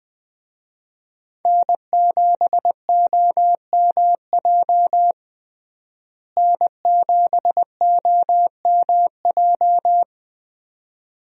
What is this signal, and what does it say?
Morse code, heard as a keyed tone, about 20 wpm, 700 Hz. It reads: N7OMJ N7OMJ
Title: USA Call Signs - 5 Characters - Repeated 2x - 20wpm